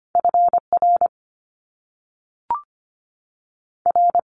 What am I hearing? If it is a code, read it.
Morse code, heard as a keyed tone, about 25 wpm, 700 Hz. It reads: F R R
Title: Top Two Letter Combinations in Words 25wpm